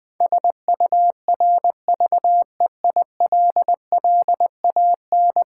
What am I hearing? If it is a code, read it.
Morse code, heard as a keyed tone, about 21 wpm, 700 Hz. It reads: SURVEILLAN
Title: Build A Word - 5 Letters At A Time - 12-Letter Words - 20wpm